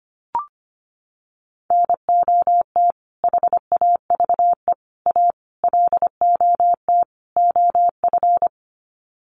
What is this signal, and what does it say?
Morse code, heard as a keyed tone, about 25 wpm, 700 Hz. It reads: NOT HAVE A LOT OF
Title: Sets of 5 Words 25wpm